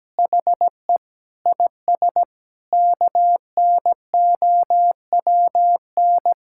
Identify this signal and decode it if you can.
Morse code, heard as a keyed tone, about 17 wpm, 700 Hz. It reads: HE IS KNOWN
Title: Sentences from Top 1600 Words 17wpm